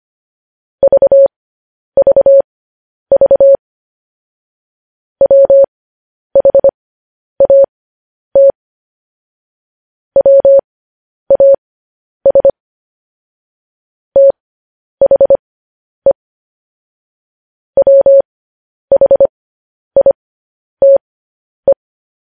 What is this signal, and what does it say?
Morse code, heard as a keyed tone, about 25 wpm, spaced as for 10 wpm, 550 Hz. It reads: VVV WHAT WAS THE WHITE
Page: News Headlines in Morse Code at 10 WPM